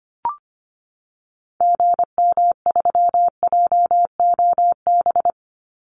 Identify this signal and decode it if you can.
Morse code, heard as a keyed tone, about 25 wpm, 700 Hz. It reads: GM3JOB
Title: Call Signs Repeated 2x - 25wpm